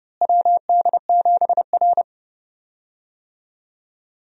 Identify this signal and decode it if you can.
Morse code, heard as a keyed tone, about 29 wpm, 700 Hz. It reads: WD7R
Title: USA Call Signs - 4 Characters - Repeated 3x - 30wpm